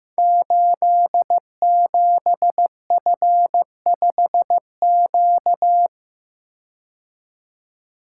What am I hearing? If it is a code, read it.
Morse code, heard as a keyed tone, about 15 wpm, 700 Hz. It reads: 87F5Q
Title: Code Groups - 5 Letter:Numbers - 15wpm